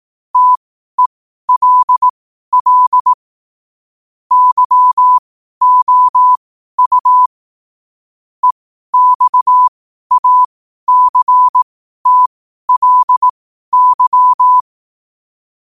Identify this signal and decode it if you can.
Morse code, heard as a keyed tone, about 18 wpm, spaced as for 12 wpm, 1000 Hz. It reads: TELL YOU EXACTLY
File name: qotdfiles / 12wpm